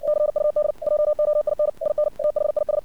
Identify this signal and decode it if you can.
Morse code, heard as a keyed tone, about 57 wpm, 600 Hz. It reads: CUD JOIN IN THIS
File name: archived / CRSnet